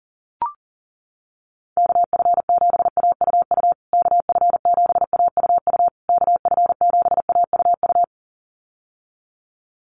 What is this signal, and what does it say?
Morse code, heard as a keyed tone, about 40 wpm, 700 Hz. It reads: KF7AUU KF7AUU KF7AUU
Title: USA Call Signs Repeated 3x 40wpm